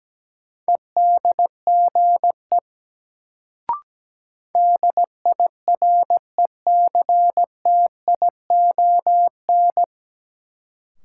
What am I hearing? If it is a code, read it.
Morse code, heard as a keyed tone, about 17 wpm, 700 Hz. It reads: EDGE DIRECTION
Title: Sentences from Top 1300 Words - New Word Drill Alt - 17wpm